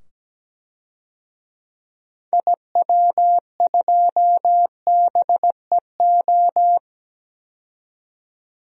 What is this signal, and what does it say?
Morse code, heard as a keyed tone, about 17 wpm, 700 Hz. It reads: IW2BEO
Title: DXCC-100 Call Signs - 6 Characters - 17wpm